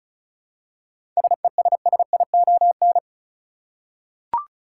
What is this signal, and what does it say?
Morse code, heard as a keyed tone, about 34 wpm, 700 Hz. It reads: SESSION
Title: Seven Letter Words - 35wpm